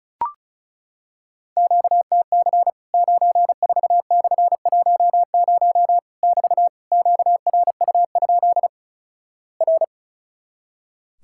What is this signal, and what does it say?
Morse code, heard as a keyed tone, about 34 wpm, 700 Hz. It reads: QTC 94/10 <BT> QRU? R
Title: Worked All Europe DX Contest - QTC Exchange - 35wpm